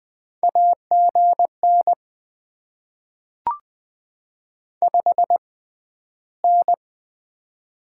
Morse code, heard as a keyed tone, about 20 wpm, spaced as for 12 wpm, 700 Hz. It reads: AGN 5N
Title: Top 20 QSO Elements - 6x Character Spacing - 20wpm